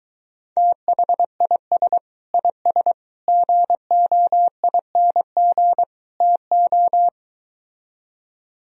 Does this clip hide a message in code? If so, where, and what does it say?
Yes: Morse code from 0.6 to 7.1 s, heard as a keyed tone, about 23 wpm, 700 Hz. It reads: THIS IS GOING TO